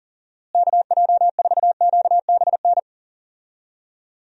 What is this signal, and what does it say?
Morse code, heard as a keyed tone, about 40 wpm, 700 Hz. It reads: KJ4QBN
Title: USA Call Signs - 6 Characters - Repeated 2x - 40wpm